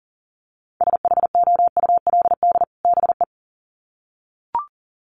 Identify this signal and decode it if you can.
Morse code, heard as a keyed tone, about 40 wpm, 700 Hz. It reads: SHOULD BE
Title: Sets of 2 Words 40wpm